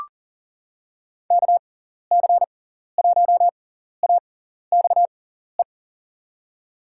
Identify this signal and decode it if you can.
Morse code, heard as a keyed tone, about 39 wpm, spaced as for 14 wpm, 700 Hz. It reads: KC1AXE
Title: USA Call Signs - 6 Characters - 6x Character Spacing - 40wpm